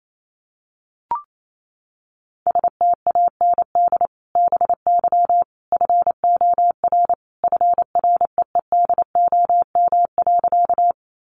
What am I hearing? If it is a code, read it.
Morse code, heard as a keyed tone, about 28 wpm, 700 Hz. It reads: STAND BY FOR FREEDOM.
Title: Sentences from Top 800 Words 28wpm